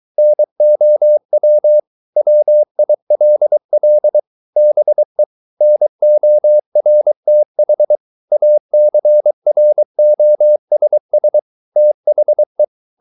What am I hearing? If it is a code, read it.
Morse code, heard as a keyed tone, about 23 wpm, 600 Hz. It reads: NOW WILL BE NORTH ACROSS THE